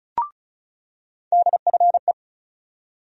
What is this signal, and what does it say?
Morse code, heard as a keyed tone, about 36 wpm, 700 Hz. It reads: DFE
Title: USA Call Sign Postfixes - 3 Letters 35wpm